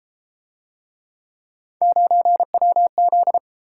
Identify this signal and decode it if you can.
Morse code, heard as a keyed tone, about 29 wpm, 700 Hz. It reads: 9WZ